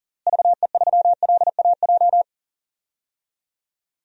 Morse code, heard as a keyed tone, about 40 wpm, 700 Hz. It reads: VE3LAJ
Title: DXCC-100 Call Signs - 6 Characters - Repeated 2x - 40wpm